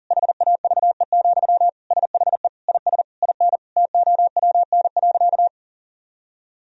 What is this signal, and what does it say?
Morse code, heard as a keyed tone, about 40 wpm, 700 Hz. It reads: HAVE, SHE IS IN TOWN.
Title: Sentences from Top 1000 Words - Review Part 4 - 40wpm